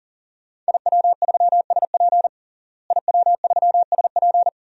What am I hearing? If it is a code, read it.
Morse code, heard as a keyed tone, about 37 wpm, 700 Hz. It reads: IW3SP IW3SP
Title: DXCC-100 Call Signs - 5 Characters - Repeated 2x - 40wpm